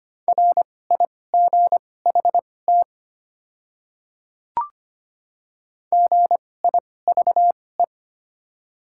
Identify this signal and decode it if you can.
Morse code, heard as a keyed tone, about 25 wpm, spaced as for 18 wpm, 700 Hz. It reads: RIGHT GIVE
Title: 2x Character Spacing - Top 500 Words - 25wpm